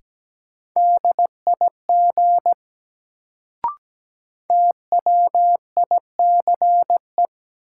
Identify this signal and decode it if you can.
Morse code, heard as a keyed tone, about 17 wpm, 700 Hz. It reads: DIG TWICE
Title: Sentences from Top 1600 Words - New Word Drill Alt - 17wpm